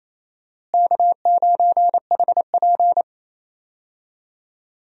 Morse code, heard as a keyed tone, about 28 wpm, 700 Hz. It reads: K9HP
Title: USA Call Signs - 4 Characters - Repeated 2x - 28wpm